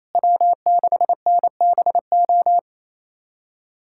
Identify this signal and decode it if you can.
Morse code, heard as a keyed tone, about 28 wpm, 700 Hz. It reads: W6NBO
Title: USA Callsign DE Callsign - 5 Characters - 28wpm